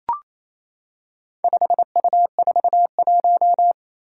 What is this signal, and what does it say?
Morse code, heard as a keyed tone, about 27 wpm, 700 Hz. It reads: 5U41
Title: Four Challenging Letter-Number Combinations 28wpm